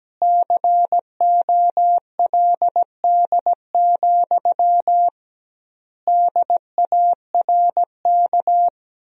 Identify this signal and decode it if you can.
Morse code, heard as a keyed tone, about 17 wpm, 700 Hz. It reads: COLD, DARK
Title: 2x Word-Spacing - Sentences from Top 500 Words - 17wpm